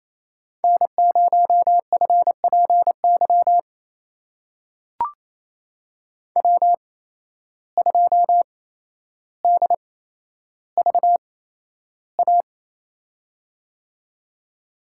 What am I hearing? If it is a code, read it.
Morse code, heard as a keyed tone, about 27 wpm, spaced as for 12 wpm, 700 Hz. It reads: N0FPY W2DVA
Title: USA Call Signs - 5 Characters - 8x Character Spacing - 28wpm